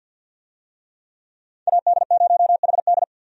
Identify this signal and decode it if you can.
Morse code, heard as a keyed tone, about 53 wpm, 700 Hz. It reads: AD0HD
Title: USA Call Signs - 5 Characters - 50wpm